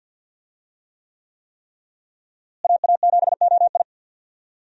Morse code, heard as a keyed tone, about 46 wpm, 700 Hz. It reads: AA7OI
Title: USA Call Signs - 5 Characters - Repeated 3x - 50wpm